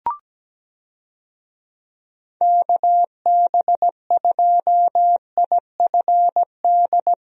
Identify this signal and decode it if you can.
Morse code, heard as a keyed tone, about 17 wpm, 700 Hz. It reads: KB2IFD
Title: USA Call Signs - 6 Characters - Repeated 2x - 17wpm